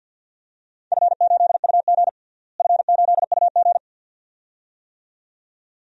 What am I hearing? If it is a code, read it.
Morse code, heard as a keyed tone, about 47 wpm, 700 Hz. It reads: F8UG F8UG
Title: DXCC-100 Call Signs - 4 Characters - Repeated 2x - 50wpm